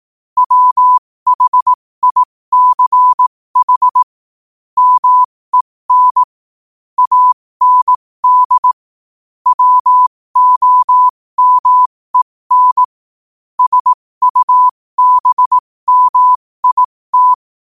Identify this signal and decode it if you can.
Morse code, heard as a keyed tone, about 18 wpm, spaced as for 15 wpm, 1000 Hz. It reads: WHICH MEN AND WOMEN SUBMIT
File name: qotdfiles / 15wpm